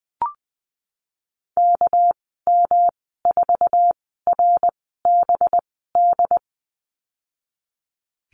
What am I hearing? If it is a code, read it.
Morse code, heard as a keyed tone, about 20 wpm, spaced as for 14 wpm, 700 Hz. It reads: KM4RBD
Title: 219 Lesson 35g - Number 0 - Call Signs 20wpm